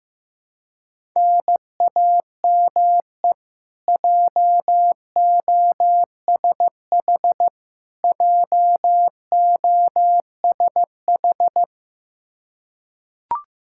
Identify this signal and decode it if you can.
Morse code, heard as a keyed tone, about 15 wpm, 700 Hz. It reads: NAME JOSH JOSH
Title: Practice QSO - Exchange 1 - Beginning - 15wpm